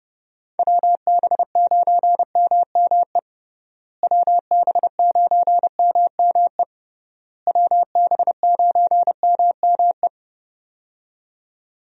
Morse code, heard as a keyed tone, about 32 wpm, 700 Hz. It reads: WB9MME WB9MME WB9MME
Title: USA Call Signs - 6 Characters - Repeated 3x - 30wpm